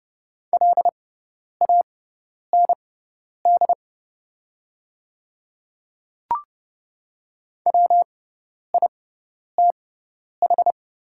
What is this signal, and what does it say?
Morse code, heard as a keyed tone, about 30 wpm, spaced as for 10 wpm, 700 Hz. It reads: LAND WITH
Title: 6x Character Spacing - Top 500 Words - 30wpm